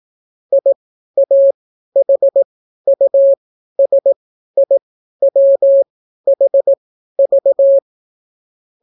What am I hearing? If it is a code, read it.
Morse code, heard as a keyed tone, about 18 wpm, spaced as for 12 wpm, 550 Hz. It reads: IAHUSIWHV